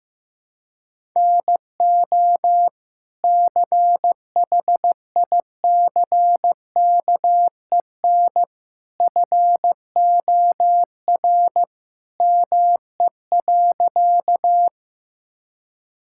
Morse code, heard as a keyed tone, about 15 wpm, 700 Hz. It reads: NO CHICKEN FOR ME.
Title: Sentences from Top 1000 Words - Review Part 5 - 15wpm